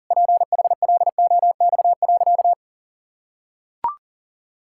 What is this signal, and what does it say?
Morse code, heard as a keyed tone, about 37 wpm, 700 Hz. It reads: PHLOX.
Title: Pangram Challenge v2 - 40wpm